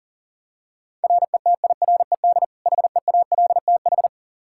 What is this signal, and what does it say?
Morse code, heard as a keyed tone, about 40 wpm, 700 Hz. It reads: RETIRED HEALTH
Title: Practice QSO - Exchange 3 - More Info - 40wpm